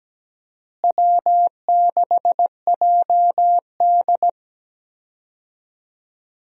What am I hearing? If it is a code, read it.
Morse code, heard as a keyed tone, about 17 wpm, 700 Hz. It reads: W6JD
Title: USA Call Signs - 4 Characters - Repeated 3x - 17wpm